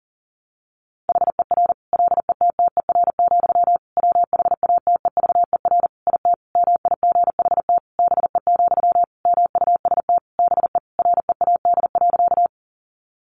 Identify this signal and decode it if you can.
Morse code, heard as a keyed tone, about 40 wpm, 700 Hz. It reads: HER LETTER, WHATEVER IT MIGHT BE, MUST BE READ.
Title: Sentences from Top 1100 Words 40wpm